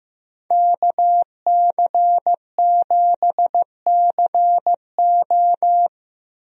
Morse code, heard as a keyed tone, about 15 wpm, 700 Hz. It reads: KC7CO
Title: USA Call Signs - 5 Characters - Repeated 2x - 15wpm